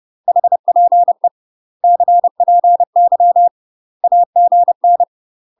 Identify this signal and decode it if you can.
Morse code, heard as a keyed tone, about 30 wpm, 700 Hz. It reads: HPE CPY AGN